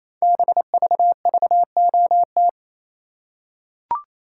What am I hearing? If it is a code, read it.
Morse code, heard as a keyed tone, about 27 wpm, 700 Hz. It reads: BVVOT
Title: Code Groups - 5 Letter:Numbers - 28wpm